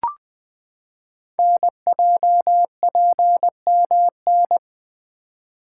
Morse code, heard as a keyed tone, about 20 wpm, 700 Hz. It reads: NJPMN
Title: Code Groups - 5 Letters - 20wpm